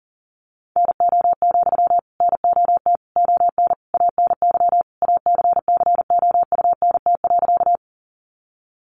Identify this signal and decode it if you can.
Morse code, heard as a keyed tone, about 40 wpm, 700 Hz. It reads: NO, NOT ON ANY ACCOUNT.
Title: Sentences from Top 900 Words 40wpm